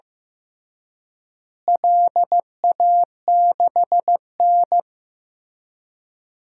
Morse code, heard as a keyed tone, about 15 wpm, 700 Hz. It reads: LA6N
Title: DXCC-100 Call Signs - 4 Characters - Repeated 3x - 15wpm